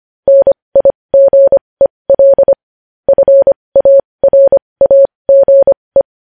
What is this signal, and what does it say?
Morse code, heard as a keyed tone, about 25 wpm, spaced as for 20 wpm, 550 Hz. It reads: NIGEL FARAGE
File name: podcast / News headlines-1x-25x20